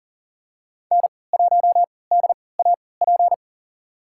Morse code, heard as a keyed tone, about 40 wpm, spaced as for 23 wpm, 700 Hz. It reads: N1DAP
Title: USA Call Signs - 3x Character Spacing - 40wpm